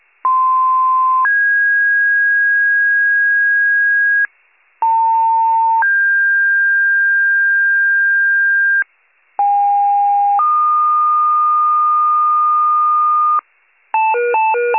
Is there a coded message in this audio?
Yes: two-tone page from 0.2 to 13.4 s, heard as a one-second tone, then a three-second tone, about 1040 then 1650 Hz, about 900 then 1600 Hz, about 810 then 1220 Hz, with noise.